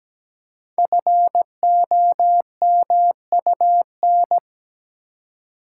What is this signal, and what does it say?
Morse code, heard as a keyed tone, about 17 wpm, 700 Hz. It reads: FOMUN
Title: Code Groups - 5 Letters - 17wpm